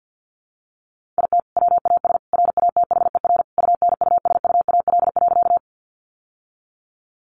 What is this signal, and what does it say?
Morse code, heard as a keyed tone, about 49 wpm, 700 Hz. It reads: IT WAS RATHER UNUSUAL.
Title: Sentences from Top 1400 Words 50wpm